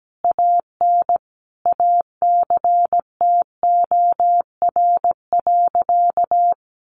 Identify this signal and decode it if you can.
Morse code, heard as a keyed tone, about 17 wpm, 700 Hz. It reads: AN ACTOR.
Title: Sentences from Top 1500 Words 17wpm